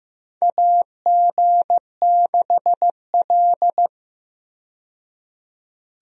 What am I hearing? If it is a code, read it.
Morse code, heard as a keyed tone, about 15 wpm, 700 Hz. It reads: AG6L
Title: USA Call Signs - 4 Characters - 15wpm